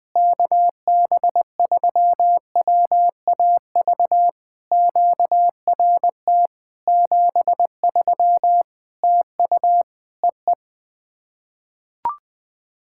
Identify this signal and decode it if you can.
Morse code, heard as a keyed tone, about 20 wpm, 700 Hz. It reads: KB3WAV QRT 73 TU EE
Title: POTA QSO Practice - 20wpm